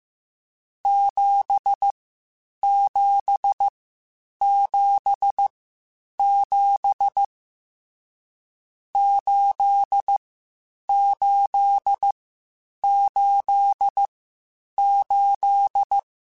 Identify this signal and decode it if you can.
Morse code, heard as a keyed tone, about 15 wpm, spaced as for 8 wpm, 800 Hz. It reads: 7777 8888